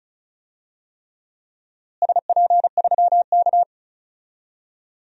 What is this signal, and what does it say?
Morse code, heard as a keyed tone, about 36 wpm, 700 Hz. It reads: SP3K
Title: DXCC-100 Call Signs - 4 Characters - Repeated 3x - 35wpm